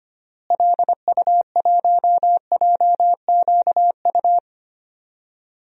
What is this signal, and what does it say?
Morse code, heard as a keyed tone, about 25 wpm, 700 Hz. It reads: LU1JQU
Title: DXCC-100 Call Signs - 6 Characters - Repeated 3x - 25wpm